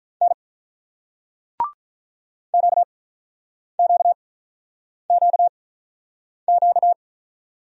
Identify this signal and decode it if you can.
Morse code, heard as a keyed tone, about 47 wpm, spaced as for 9 wpm, 700 Hz. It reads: N QQQQ